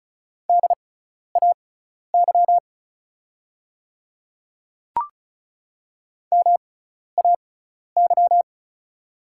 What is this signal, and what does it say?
Morse code, heard as a keyed tone, about 36 wpm, spaced as for 12 wpm, 700 Hz. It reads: DAY MAY
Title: 6x Character Spacing - Top 100 Words - 35wpm